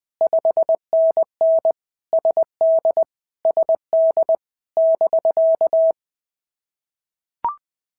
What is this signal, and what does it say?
Morse code, heard as a keyed tone, about 20 wpm, 650 Hz. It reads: 5NN SD SD <BK>
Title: POTA QSO Practice - 20wpm